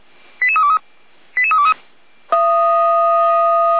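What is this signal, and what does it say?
Five-tone selective call (ZVEI), heard as quick runs of five tones, tones of about 80 ms. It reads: 88323 88323
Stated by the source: ZVEI five-tone call received by SDR, from the Feuerwehr Orlamünde fire brigade server.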